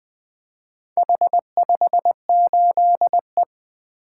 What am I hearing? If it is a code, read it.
Morse code, heard as a keyed tone, about 20 wpm, 700 Hz. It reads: H58E
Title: Four Letter-Number Combinations - 20wpm